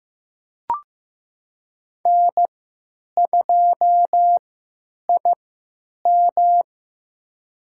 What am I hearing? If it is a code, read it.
Morse code, heard as a keyed tone, about 15 wpm, spaced as for 9 wpm, 700 Hz. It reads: N2IM
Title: USA Call Signs - 4 Characters - 3x Character Spacing - 15wpm